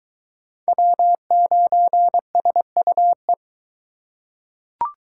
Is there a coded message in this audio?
Yes: Morse code from 0.7 to 3.3 s, heard as a keyed tone, about 22 wpm, 700 Hz. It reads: W9SUE